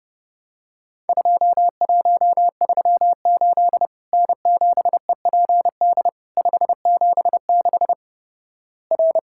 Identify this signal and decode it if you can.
Morse code, heard as a keyed tone, about 29 wpm, 700 Hz. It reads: 2138 N7EPD 576 R
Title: Worked All Europe DX Contest - QTC Exchange - 30wpm